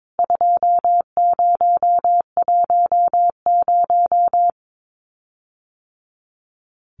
Morse code, heard as a keyed tone, about 22 wpm, 700 Hz. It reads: 2010
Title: CWT Exchange 22wpm